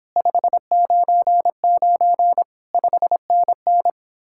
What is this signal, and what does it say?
Morse code, heard as a keyed tone, about 29 wpm, 700 Hz. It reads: 599 5NN